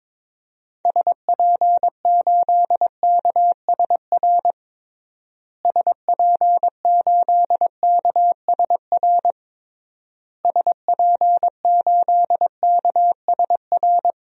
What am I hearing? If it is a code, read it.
Morse code, heard as a keyed tone, about 22 wpm, 700 Hz. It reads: SP8KSR SP8KSR SP8KSR
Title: DXCC-100 Call Signs - 6 Characters - Repeated 3x - 22wpm